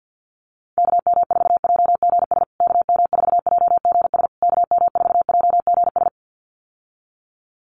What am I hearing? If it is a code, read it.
Morse code, heard as a keyed tone, about 50 wpm, 700 Hz. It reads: KM4JGS KM4JGS KM4JGS
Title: USA Call Signs Repeated 3x 50wpm